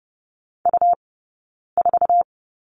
Morse code, heard as a keyed tone, about 30 wpm, spaced as for 9 wpm, 690 Hz. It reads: U4
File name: Lesson 17d - Letter u - Two Characters 30wpm